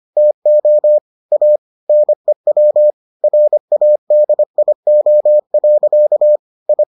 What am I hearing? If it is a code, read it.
Morse code, heard as a keyed tone, about 25 wpm, 600 Hz. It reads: TO A NEW RADIO. I